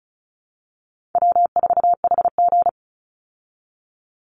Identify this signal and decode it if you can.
Morse code, heard as a keyed tone, about 34 wpm, 700 Hz. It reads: W4HG
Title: Easy Call Signs Repeated 3x - 35wpm